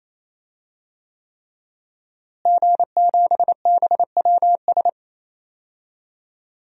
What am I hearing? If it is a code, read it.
Morse code, heard as a keyed tone, about 28 wpm, 700 Hz. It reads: G7BWS
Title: DXCC-100 Call Signs - 5 Characters - Repeated 3x - 28wpm